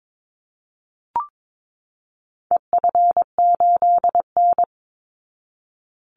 Morse code, heard as a keyed tone, about 22 wpm, 700 Hz. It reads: EF8N
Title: Call Signs Repeated 3x - 22wpm